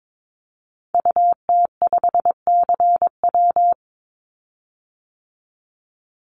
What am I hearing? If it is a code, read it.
Morse code, heard as a keyed tone, about 22 wpm, 700 Hz. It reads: UT5CW
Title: Call Signs Repeated 2x - 22wpm